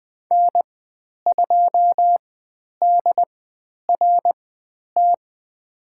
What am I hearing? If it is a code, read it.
Morse code, heard as a keyed tone, about 20 wpm, spaced as for 10 wpm, 700 Hz. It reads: N2DRT
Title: 095 Lesson 16g - Letter c - Call Signs 20wpm